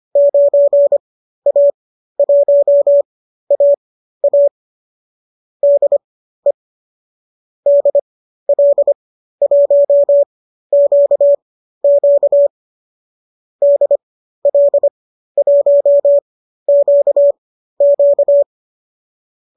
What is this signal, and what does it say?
Morse code, heard as a keyed tone, about 25 wpm, spaced as for 13 wpm, 590 Hz. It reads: 9A1AA DE DL1QQ DL1QQ
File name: qso219 13